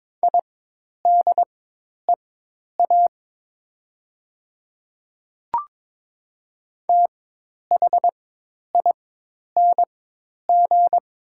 Morse code, heard as a keyed tone, about 22 wpm, spaced as for 10 wpm, 700 Hz. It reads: IDEA THING
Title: 4x Character Spacing - Top 300 Words - 22wpm